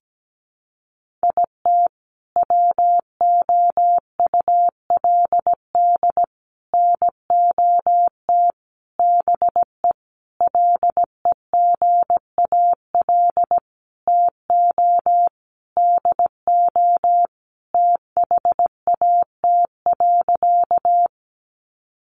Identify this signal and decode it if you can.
Morse code, heard as a keyed tone, about 17 wpm, 700 Hz. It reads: IT WOULD NOT BE LEGAL TO DO THAT.
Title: Sentences from Top 1200 Words 17wpm